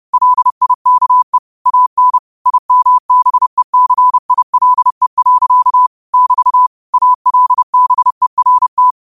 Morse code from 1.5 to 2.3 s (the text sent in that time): AN